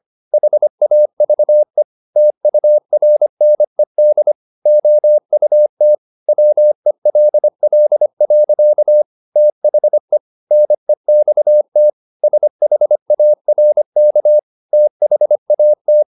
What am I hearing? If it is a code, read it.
Morse code, heard as a keyed tone, about 25 wpm, 600 Hz. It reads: HAVE TURNED OUT WELL. THE NEXT SHARK THAT